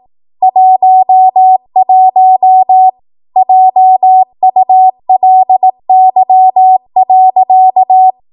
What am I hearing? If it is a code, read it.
Morse code, heard as a keyed tone, about 18 wpm, 750 Hz. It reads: 11 JULY.
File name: Morse / 18WPM